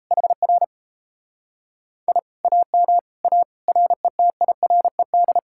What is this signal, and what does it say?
Morse code, heard as a keyed tone, about 37 wpm, 700 Hz. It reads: HR I AM A RETIRED